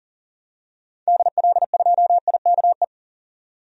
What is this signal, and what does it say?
Morse code, heard as a keyed tone, about 40 wpm, 700 Hz. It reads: DL2IKE
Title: DXCC-100 Call Signs - 6 Characters - 40wpm